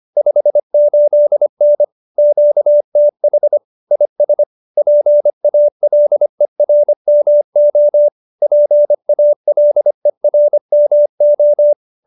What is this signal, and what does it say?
Morse code, heard as a keyed tone, about 25 wpm, 590 Hz. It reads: 58N QTH IS PALERMO PALERMO